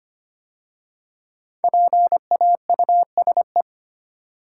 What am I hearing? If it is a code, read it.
Morse code, heard as a keyed tone, about 25 wpm, 700 Hz. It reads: PAUSE
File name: 5 Letter Words - First 3 Letters Given - 25wpm